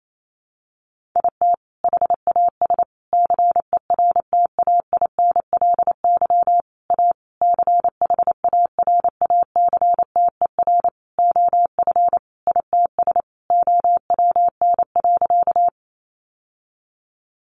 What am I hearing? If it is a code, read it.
Morse code, heard as a keyed tone, about 28 wpm, 700 Hz. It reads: IT HAS CERTAINLY A CHARACTER OF ITS OWN.
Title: Sentences from Top 1200 Words 28wpm